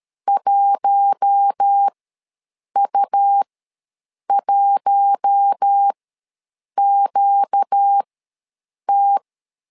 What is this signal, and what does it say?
Morse code, heard as a keyed tone, about 13 wpm, spaced as for 7 wpm, 800 Hz. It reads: 1U1QT
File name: LEZIONI / 4 Q1 ws 12wpm 2129